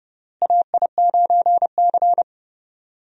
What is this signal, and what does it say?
Morse code, heard as a keyed tone, about 32 wpm, 700 Hz. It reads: AI9C
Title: USA Call Signs - 4 Characters - Repeated 3x - 30wpm